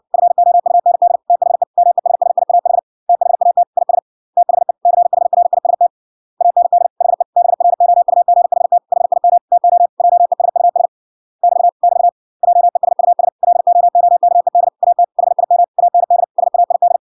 Morse code, heard as a keyed tone, about 57 wpm, 700 Hz. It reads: FOUND THE MESSIAH THAT IS THE CHRIST AND HE BROUGHT HIM TO JESUS <BT> <BT> JESUS LOOKED AT HIM AND SAID